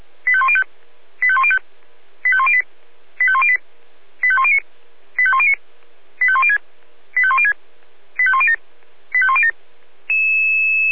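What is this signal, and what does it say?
Five-tone selective call (ZVEI), heard as quick runs of five tones, tones of about 70 ms. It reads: 85105 85105 85108 85108 85109 85109 85106 85106 85107 85107
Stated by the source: ZVEI five-tone call received on an SDR.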